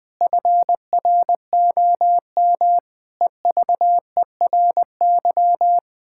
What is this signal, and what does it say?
Morse code, heard as a keyed tone, about 20 wpm, 700 Hz. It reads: FROM EVERY
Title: Pangram Challenge v2 - 20wpm